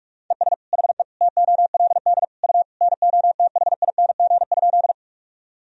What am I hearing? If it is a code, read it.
Morse code, heard as a keyed tone, about 46 wpm, 700 Hz. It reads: ES HE TOLD U NOTHING?
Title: Sentences from Top 500 Words and Abbreviations 45wpm